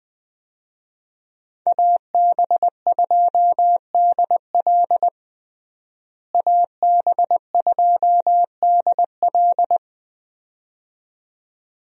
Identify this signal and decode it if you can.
Morse code, heard as a keyed tone, about 20 wpm, 700 Hz. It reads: AB2DL AB2DL
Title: USA Call Signs - 5 Characters - Repeated 2x - 20wpm